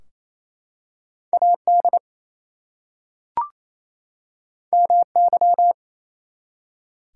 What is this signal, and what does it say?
Morse code, heard as a keyed tone, about 27 wpm, 700 Hz. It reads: AD MY
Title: Two Letter Words - 28wpm